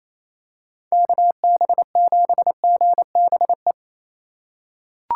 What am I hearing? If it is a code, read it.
Morse code, heard as a keyed tone, about 27 wpm, 700 Hz. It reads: KB7GBE
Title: USA Call Signs - 2x Character Spacing - 28wpm